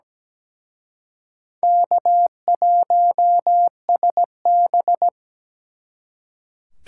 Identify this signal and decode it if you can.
Morse code, heard as a keyed tone, about 20 wpm, 700 Hz. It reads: K1SB
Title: Speed Racing - USA 1x2 Call Signs 17wpm